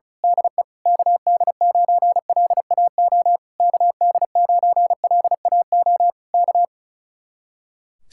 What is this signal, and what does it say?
Morse code, heard as a keyed tone, about 33 wpm, 700 Hz. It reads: DE KD9LAO KD9LAO K
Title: Speed-Racing Common QSO Phrases Encore - 35wpm